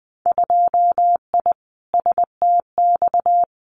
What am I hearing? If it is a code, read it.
Morse code, heard as a keyed tone, about 20 wpm, 700 Hz. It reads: 2I STX
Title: Winter Field Day Exchanges 20wpm